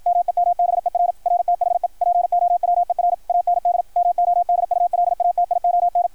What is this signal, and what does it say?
Morse code, heard as a keyed tone, about 53 wpm, 690 Hz, with radio noise. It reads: MEMBER RTHE POWER AND MODULATION
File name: archived / CRSnet